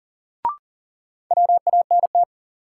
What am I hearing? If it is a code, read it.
Morse code, heard as a keyed tone, about 40 wpm, 700 Hz. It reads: WANT
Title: Sets of 2 Words - New Word Drill - 40wpm